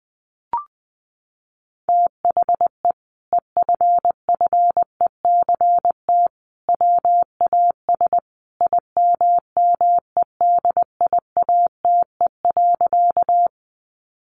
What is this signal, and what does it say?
Morse code, heard as a keyed tone, about 20 wpm, 700 Hz. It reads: THE EFFECT WAS IMMEDIATE.
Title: Sentences from Top 1200 Words 20wpm